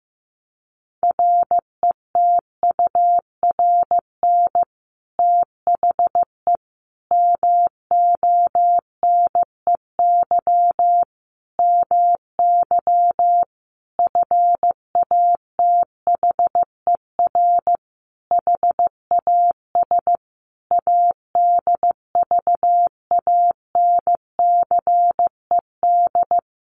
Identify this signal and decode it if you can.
Morse code, heard as a keyed tone, about 15 wpm, 700 Hz. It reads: RETURN THE MONEY MY FATHER HAS ADVANCED
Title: Sentences from Top 1200 Words 15wpm